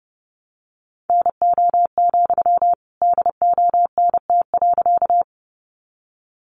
Morse code, heard as a keyed tone, about 28 wpm, 700 Hz. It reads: NO, DONT.
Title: Sentences from Top 300 Words 30wpm